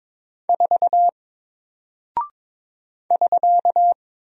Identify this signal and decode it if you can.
Morse code, heard as a keyed tone, about 22 wpm, 700 Hz. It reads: 4 <SK>
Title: Single Character (Letter:Number:Punctuation:Prosign) 22wpm